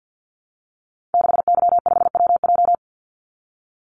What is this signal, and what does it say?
Morse code, heard as a keyed tone, about 50 wpm, 700 Hz. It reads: 6Y5WJ
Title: Call Signs Repeated 3x - 50wpm